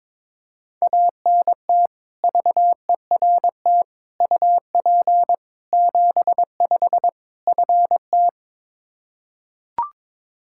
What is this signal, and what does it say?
Morse code, heard as a keyed tone, about 22 wpm, 700 Hz. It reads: ANT VERT UP 75 FT
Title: K1USN SST QSO Practice 22wpm